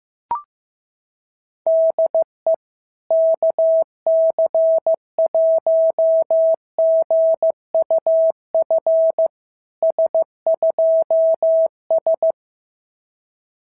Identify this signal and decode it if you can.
Morse code, heard as a keyed tone, about 15 wpm, 650 Hz. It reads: DE KC1GUF S2S
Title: SOTA QSO Practice 15wpm